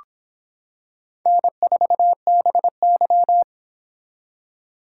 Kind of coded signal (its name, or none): Morse code